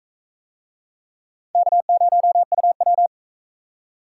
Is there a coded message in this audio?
Yes: Morse code from 1.6 to 3.1 s, heard as a keyed tone, about 40 wpm, 700 Hz. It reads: K0UW